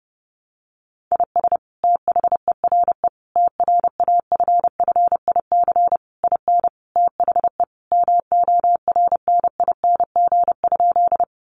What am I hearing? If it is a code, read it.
Morse code, heard as a keyed tone, about 30 wpm, 700 Hz. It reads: IS THERE TRAFFIC IN THE MORNING?
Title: Sentences from Top 1300 Words 30wpm